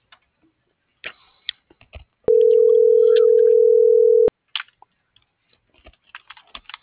Ringback tone (North America), 440 + 480 Hz.